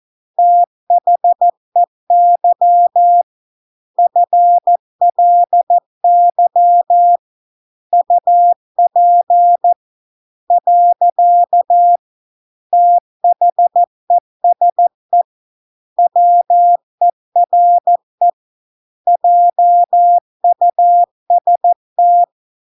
Morse code, heard as a keyed tone, about 14 wpm, 700 Hz. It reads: THEY FLY UP . THESE WERE JUST